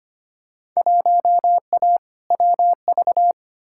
Morse code, heard as a keyed tone, about 25 wpm, 700 Hz. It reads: 1A WV
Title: Field Day Exchanges Encore - 25wpm